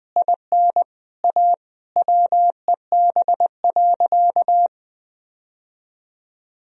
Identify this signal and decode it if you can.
Morse code, heard as a keyed tone, about 20 wpm, 700 Hz. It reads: IN A WEB.
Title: Sentences from Top 1000 Words - Review Part 4 - 20wpm